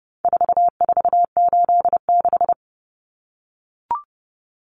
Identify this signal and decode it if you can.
Morse code, heard as a keyed tone, about 30 wpm, 700 Hz. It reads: 4486
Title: Numbers - 4-Digit 30wpm